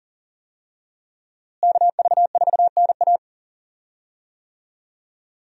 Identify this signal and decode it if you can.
Morse code, heard as a keyed tone, about 40 wpm, 700 Hz. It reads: KV4NA
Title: USA Call Signs - 5 Characters - Repeated 2x - 40wpm